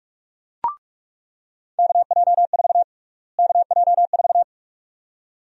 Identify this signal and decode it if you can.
Morse code, heard as a keyed tone, about 48 wpm, 700 Hz. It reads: KJ4 KJ4
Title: USA Call Sign Prefixes - Repeated 2x - 45wpm